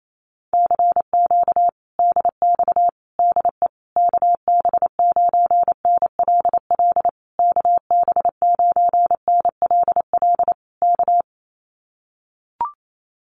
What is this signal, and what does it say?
Morse code, heard as a keyed tone, about 28 wpm, 700 Hz. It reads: CQ DX DE KB9NLL KB9NLL K
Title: Common QSO Phrases Encore 28wpm